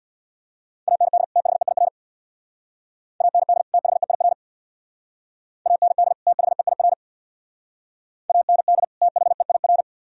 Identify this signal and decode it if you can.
Morse code, heard as a keyed tone, about 64 wpm, 690 Hz. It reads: AND THEIR AND THEIR AND THEIR AND THEIR